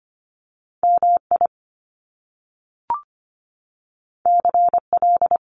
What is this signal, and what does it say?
Morse code, heard as a keyed tone, about 25 wpm, 700 Hz. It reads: MI CL
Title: Top Two Letter Combinations in Words 25wpm